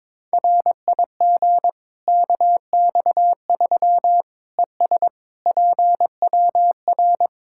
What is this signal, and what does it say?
Morse code, heard as a keyed tone, about 22 wpm, 700 Hz. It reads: RIG KX3 ES PWR
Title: Practice QSO - Exchange 2 - Info - 22wpm